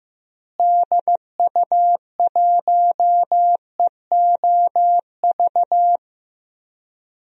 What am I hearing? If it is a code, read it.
Morse code, heard as a keyed tone, about 16 wpm, 700 Hz. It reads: DU1EOV
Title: DXCC-100 Call Signs - 6 Characters - Repeated 2x - 15wpm